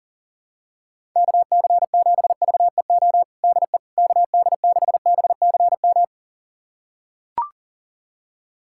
Morse code, heard as a keyed tone, about 40 wpm, 700 Hz. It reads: KC7VEO DE KD6BCM
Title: USA Callsign DE Callsign - 6 Characters - 40wpm